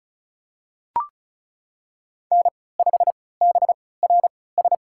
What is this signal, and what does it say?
Morse code, heard as a keyed tone, about 36 wpm, spaced as for 20 wpm, 700 Hz. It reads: N5BRS
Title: USA Call Signs - 3x Character Spacing - 35wpm